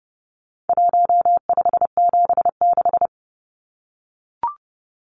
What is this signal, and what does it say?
Morse code, heard as a keyed tone, about 30 wpm, 700 Hz. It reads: 1576
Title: Numbers - Contest Style - 4-Digit 30wpm